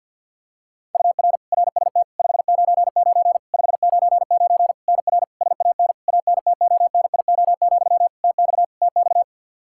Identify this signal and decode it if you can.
Morse code, heard as a keyed tone, about 50 wpm, 700 Hz. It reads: UR RST 599 599 NR SAN ANTONIO, TX TX
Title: K1USN SST QSO Practice 50wpm